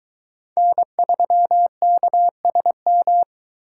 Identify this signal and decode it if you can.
Morse code, heard as a keyed tone, about 22 wpm, 700 Hz. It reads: N3KSM